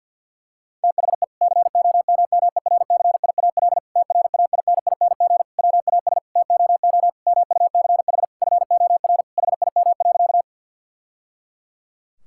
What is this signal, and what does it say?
Morse code, heard as a keyed tone, about 51 wpm, 690 Hz. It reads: THE COMMERCIAL TRAINING WAS TOO MUCH FOR HIM.
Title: Sentences from Top 1000 Words - Review Part 5 - 50wpm